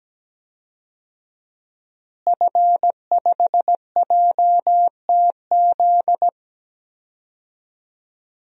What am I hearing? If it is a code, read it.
Morse code, heard as a keyed tone, about 17 wpm, 700 Hz. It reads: F5JTZ
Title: DXCC-100 Call Signs - 5 Characters - Repeated 2x - 17wpm